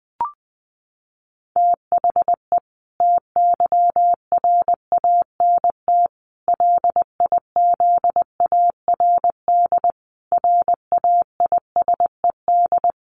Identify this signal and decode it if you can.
Morse code, heard as a keyed tone, about 20 wpm, 700 Hz. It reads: THE TYRANT LIZARD RAISED
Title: A Sound Of Thunder 20wpm